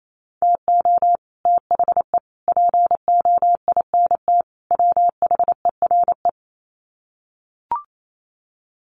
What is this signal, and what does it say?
Morse code, heard as a keyed tone, about 28 wpm, 700 Hz. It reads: TO THE POINT WHERE
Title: Sets of 4 Words 28wpm